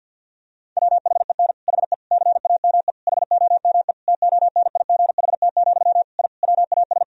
Morse code, heard as a keyed tone, about 50 wpm, 700 Hz. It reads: WHEN HE CAME HOME TONIGHT, I WAS
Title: Sentences from Top 1000 Words - Review Part 3 - 50wpm